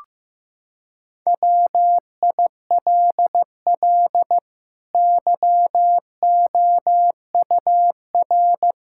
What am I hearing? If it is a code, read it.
Morse code, heard as a keyed tone, about 15 wpm, 700 Hz. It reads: WILL YOUR
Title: Sentences from Top 1000 Words - Review Part 5 - 15wpm